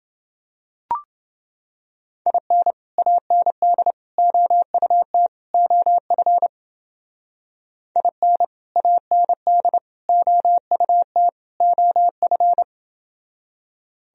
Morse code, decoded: IN AND OUT OF IN AND OUT OF

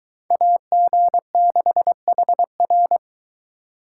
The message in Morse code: AG6HR

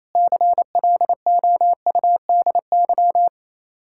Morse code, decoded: CLOUDY

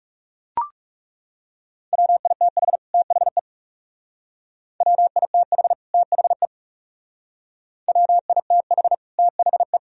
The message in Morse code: WITH THE WITH THE WITH THE

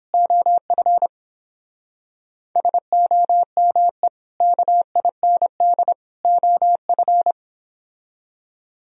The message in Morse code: OF SOME KIND OF

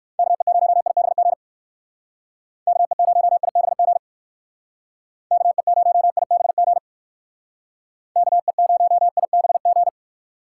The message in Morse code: KE0IBG KE0IBG KE0IBG KE0IBG

